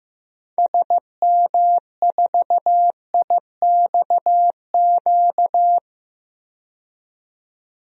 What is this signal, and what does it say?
Morse code, heard as a keyed tone, about 15 wpm, 700 Hz. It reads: SM4IXQ
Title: DXCC-100 Call Signs - 6 Characters - 15wpm